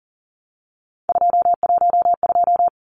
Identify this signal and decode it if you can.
Morse code, heard as a keyed tone, about 40 wpm, 700 Hz. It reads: 212
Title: Numbers - 3-Digit 40wpm